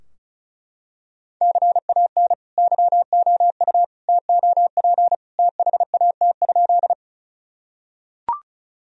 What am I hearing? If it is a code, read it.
Morse code, heard as a keyed tone, about 36 wpm, 700 Hz. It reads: CAN YOU TOP THAT?
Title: Sentences from Top 1000 Words - Review Part 2 - 35wpm